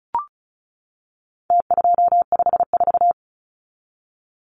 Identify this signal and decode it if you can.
Morse code, heard as a keyed tone, about 34 wpm, 700 Hz. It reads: T254
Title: Numbers - Contest Style - 4-Digit 35wpm